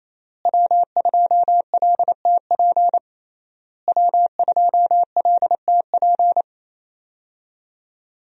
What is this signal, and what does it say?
Morse code, heard as a keyed tone, about 27 wpm, 700 Hz. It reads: W2LTP W2LTP
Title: USA Call Signs - 5 Characters - Repeated 3x - 28wpm